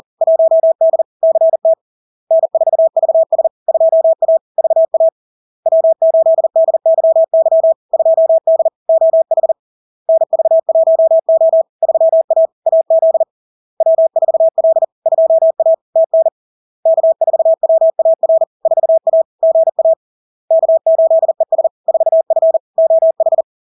Morse code, heard as a keyed tone, about 40 wpm, 650 Hz. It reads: 1D CT N4VS 2A VA W8DYY 2D OH NV1O 3A AZ W4L 2A TN K4WAR 4A GA K8ES 4F OH
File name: LARK-FD-2022-Log-in-CW-40wpm-650-Hz